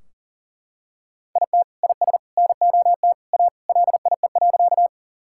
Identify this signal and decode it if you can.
Morse code, heard as a keyed tone, about 38 wpm, 700 Hz. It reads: IT IS NOT A LIE.
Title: Sentences from Top 1600 Words 40wpm